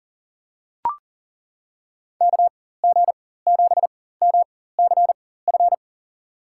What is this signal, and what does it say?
Morse code, heard as a keyed tone, about 40 wpm, spaced as for 19 wpm, 700 Hz. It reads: KG7MCF
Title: USA Call Signs - 6 Characters - 4x Character Spacing - 40wpm